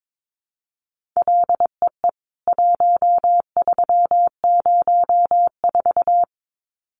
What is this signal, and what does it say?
Morse code, heard as a keyed tone, about 22 wpm, 700 Hz. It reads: LEE 1304
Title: CWT Exchange 22wpm